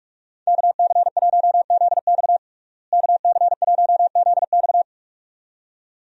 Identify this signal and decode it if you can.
Morse code, heard as a keyed tone, about 46 wpm, 700 Hz. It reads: KC1ZX KC1ZX
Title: USA Call Signs - 5 Characters - Repeated 2x - 45wpm